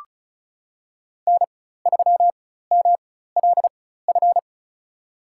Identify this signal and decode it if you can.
Morse code, heard as a keyed tone, about 36 wpm, spaced as for 16 wpm, 700 Hz. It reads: N3MLF
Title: USA Call Signs - 5 Characters - 4x Character Spacing - 35wpm